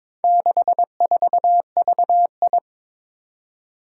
Morse code, heard as a keyed tone, about 22 wpm, 700 Hz. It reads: 64VI
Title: Four Challenging Letter-Number Combinations 22wpm